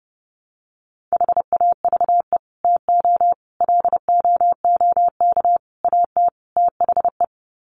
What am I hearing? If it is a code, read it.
Morse code, heard as a keyed tone, about 30 wpm, 700 Hz. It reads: HAVE TO LOOK AT THE
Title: Sets of 5 Words Encore 30wpm